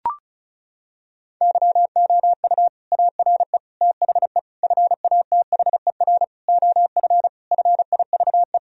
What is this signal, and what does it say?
Morse code, heard as a keyed tone, about 34 wpm, 700 Hz. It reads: YOU ARE THE FATHER OF FIVE
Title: Sentences from Top 1000 Words - Review Part 2 - 35wpm